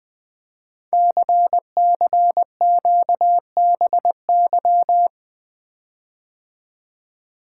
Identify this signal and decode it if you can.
Morse code, heard as a keyed tone, about 20 wpm, 700 Hz. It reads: CCQBY
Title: Code Groups - 5 Letters - 20wpm